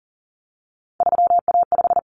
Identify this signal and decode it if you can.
Morse code, heard as a keyed tone, about 40 wpm, 700 Hz. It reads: 3A5
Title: Common Cut Numbers - 3-Digit 40wpm